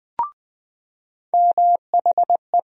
Morse code, heard as a keyed tone, about 20 wpm, 700 Hz. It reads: MHE
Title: USA Call Sign Postfixes - 3 Letters 20wpm